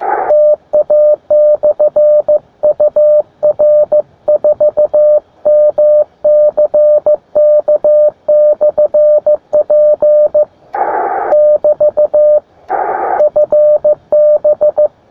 Morse code, heard as a keyed tone, about 15 wpm, 600 Hz, with radio noise. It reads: TA/UR4MCK/P <BT> FB
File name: attachments / sp4ipl 20m